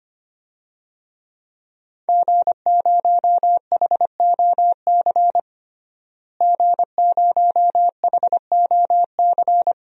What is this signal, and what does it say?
Morse code, heard as a keyed tone, about 25 wpm, 700 Hz. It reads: G0HOC G0HOC
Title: DXCC-100 Call Signs - 5 Characters - Repeated 3x - 25wpm